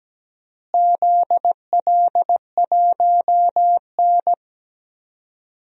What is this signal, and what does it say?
Morse code, heard as a keyed tone, about 17 wpm, 700 Hz. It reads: ZL1N
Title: DXCC-100 Call Signs - 4 Characters - Repeated 3x - 17wpm